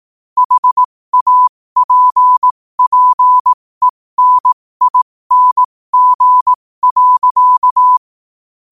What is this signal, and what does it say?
Morse code, heard as a keyed tone, about 18 wpm, spaced as for 15 wpm, 1000 Hz. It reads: HAPPENING.
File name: qotdfiles / 15wpm